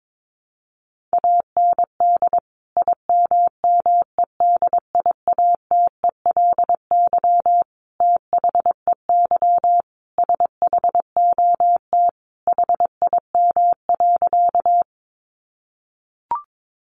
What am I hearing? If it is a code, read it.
Morse code, heard as a keyed tone, about 22 wpm, 700 Hz. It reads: AND IMMEDIATELY THEY SHOT HIM.
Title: Sentences from Top 1100 Words 22wpm